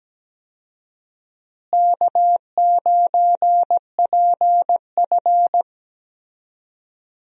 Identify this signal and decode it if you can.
Morse code, heard as a keyed tone, about 17 wpm, 700 Hz. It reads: K9PF
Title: USA Call Signs - 4 Characters - Repeated 3x - 17wpm